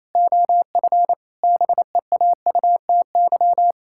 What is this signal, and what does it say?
Morse code, heard as a keyed tone, about 28 wpm, 700 Hz. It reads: OF BEAUTY
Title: Pangram Challenge v2 - 28wpm